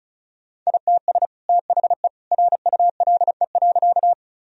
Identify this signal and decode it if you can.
Morse code, heard as a keyed tone, about 36 wpm, 700 Hz. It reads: ITS THE RULE.
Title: Sentences from Top 1000 Words - Review Part 2 - 35wpm